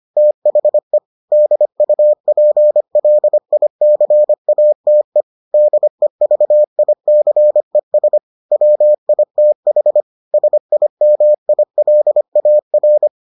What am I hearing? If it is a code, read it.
Morse code, heard as a keyed tone, about 25 wpm, 600 Hz. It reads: THE DUPLICATE DEVICES WITH SIMILAR